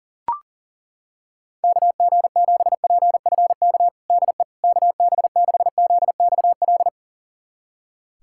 Morse code, heard as a keyed tone, about 40 wpm, 700 Hz. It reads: KG7PFK DE KB6ZXL
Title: USA Callsign DE Callsign - 6 Characters - 40wpm